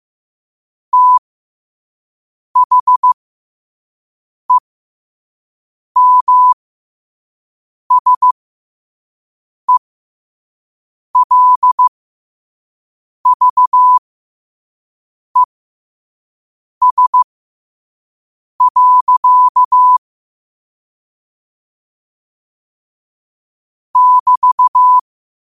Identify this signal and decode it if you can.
Morse code, heard as a keyed tone, about 15 wpm, spaced as for 5 wpm, 1000 Hz. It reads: THEMSELVES. <BT>